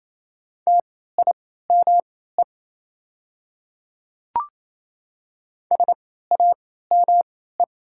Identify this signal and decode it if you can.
Morse code, heard as a keyed tone, about 28 wpm, spaced as for 16 wpm, 690 Hz. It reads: TIME SAME
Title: 3x Character Spacing - Top 200 Words - 28wpm